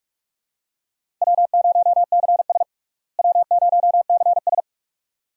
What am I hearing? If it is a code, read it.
Morse code, heard as a keyed tone, about 41 wpm, 700 Hz. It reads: W0CS W0CS
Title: USA Call Signs - 4 Characters - Repeated 2x - 45wpm